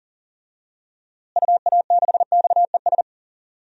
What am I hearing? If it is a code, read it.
Morse code, heard as a keyed tone, about 38 wpm, 700 Hz. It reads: UA6XES
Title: DXCC-100 Call Signs - 6 Characters - Repeated 2x - 40wpm